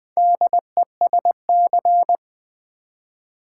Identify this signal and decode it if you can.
Morse code, heard as a keyed tone, about 20 wpm, 700 Hz. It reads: DESC
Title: Build A Word - 4 Letters At A Time - 11-Letter Words - 20wpm